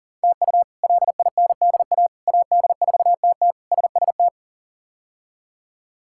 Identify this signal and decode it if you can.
Morse code, heard as a keyed tone, about 40 wpm, 700 Hz. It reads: TU LINDA AD4TT SST
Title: K1USN SST QSO Practice - 40wpm